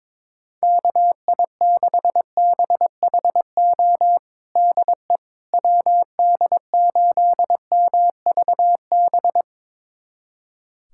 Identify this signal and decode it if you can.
Morse code, heard as a keyed tone, about 22 wpm, 700 Hz. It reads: KI6BHO DE WD8MVB
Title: USA Callsign DE Callsign 22wpm